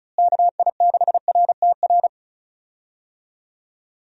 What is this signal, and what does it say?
Morse code, heard as a keyed tone, about 34 wpm, 700 Hz. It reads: KI6RTR
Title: USA Call Signs - 6 Characters - Repeated 2x - 35wpm